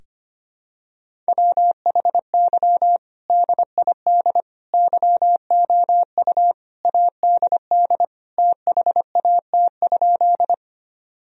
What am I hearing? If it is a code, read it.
Morse code, heard as a keyed tone, about 25 wpm, 700 Hz. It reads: WHY DID YOU ADD THAT?
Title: Sentences from Top 1000 Words - Review Part 1 - 25wpm